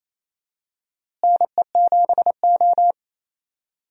Morse code, heard as a keyed tone, about 28 wpm, 700 Hz. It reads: NE7O
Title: USA Call Signs - 4 Characters - Repeated 2x - 28wpm